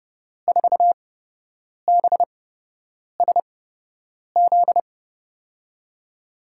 Morse code, heard as a keyed tone, about 32 wpm, spaced as for 8 wpm, 700 Hz. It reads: 4BSZ
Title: USA Call Signs - 6 Characters - 8x Character Spacing - 30wpm